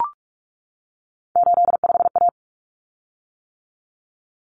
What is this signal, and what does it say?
Morse code, heard as a keyed tone, about 44 wpm, 700 Hz. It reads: 85A
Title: Common Cut Numbers - 3-Digit 45wpm